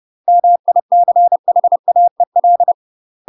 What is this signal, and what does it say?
Morse code, heard as a keyed tone, about 30 wpm, 700 Hz. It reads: MICHAEL
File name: UK MASTER QSOs TRAINING / 30WPM / UK MASTER QSOs 0047